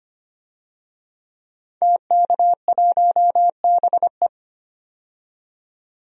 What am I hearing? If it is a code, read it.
Morse code, heard as a keyed tone, about 25 wpm, 700 Hz. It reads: TK1BE
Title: DXCC-100 Call Signs - 5 Characters - 25wpm